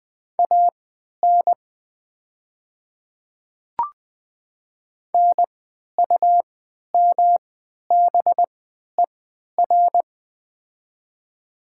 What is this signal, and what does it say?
Morse code, heard as a keyed tone, about 20 wpm, spaced as for 11 wpm, 700 Hz. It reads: AN NUMBER
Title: 3x Character Spacing - Top 100 Words - 20wpm